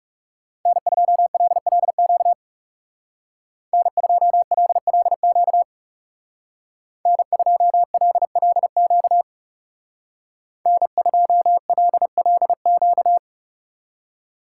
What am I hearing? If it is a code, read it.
Morse code, heard as a keyed tone, about 36 wpm, 700 Hz. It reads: N2LLQ N2LLQ N2LLQ N2LLQ
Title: Speed Racing - USA 1x3 Call Signs 30wpm